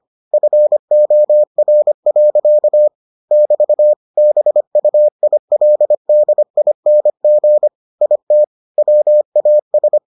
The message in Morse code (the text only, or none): FOR. <BT> BUILDING IT WAS